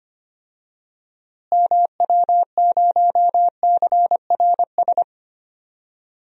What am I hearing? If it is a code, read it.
Morse code, heard as a keyed tone, about 25 wpm, 700 Hz. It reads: MW0CRS
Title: DXCC-100 Call Signs - 6 Characters - Repeated 2x - 25wpm